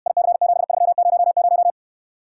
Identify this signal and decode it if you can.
Morse code, heard as a keyed tone, about 69 wpm, 700 Hz. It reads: IC7300